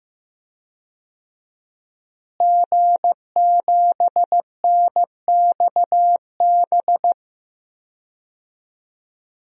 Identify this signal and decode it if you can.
Morse code, heard as a keyed tone, about 16 wpm, 700 Hz. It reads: G7NXB
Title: DXCC-100 Call Signs - 5 Characters - Repeated 2x - 15wpm